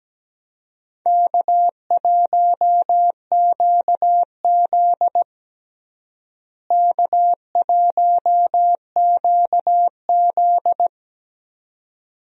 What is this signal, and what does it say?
Morse code, heard as a keyed tone, about 17 wpm, 700 Hz. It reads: K1QZ K1QZ
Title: USA Call Signs - 4 Characters - Repeated 2x - 17wpm